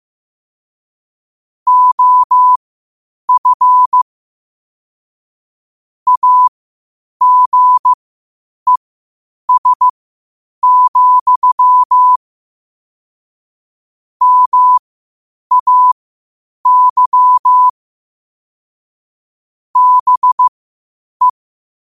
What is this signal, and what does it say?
Morse code, heard as a keyed tone, about 15 wpm, spaced as for 8 wpm, 1000 Hz. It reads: OF AGES, MAY BE
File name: qotdfiles / 8wpm